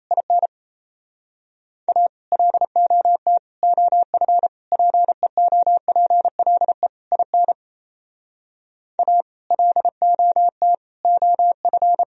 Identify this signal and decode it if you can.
Morse code, heard as a keyed tone, about 37 wpm, 700 Hz. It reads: IN A LOT OF PEOPLE IN A LOT OF